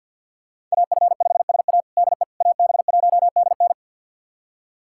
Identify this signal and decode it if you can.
Morse code, heard as a keyed tone, about 52 wpm, 690 Hz. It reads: AF5SA DE AB1DN
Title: USA Callsign DE Callsign - 5 Characters - 50wpm